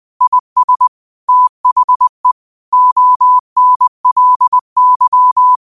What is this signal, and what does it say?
Morse code, heard as a keyed tone, about 20 wpm, 1000 Hz. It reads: IS THE ONLY